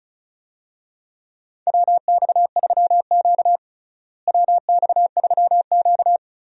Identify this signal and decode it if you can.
Morse code, heard as a keyed tone, about 34 wpm, 700 Hz. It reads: WX3Q WX3Q
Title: USA Call Signs - 4 Characters - Repeated 2x - 35wpm